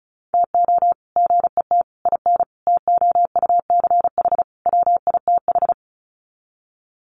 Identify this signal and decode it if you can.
Morse code, heard as a keyed tone, about 34 wpm, 700 Hz. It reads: TO GET IN TOUCH WITH
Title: Sets of 5 Words Encore2 35wpm